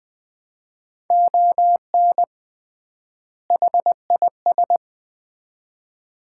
Morse code, heard as a keyed tone, about 20 wpm, 700 Hz. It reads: ON HIS
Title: 3x Word-Spacing - Sets of 2 Words - 20wpm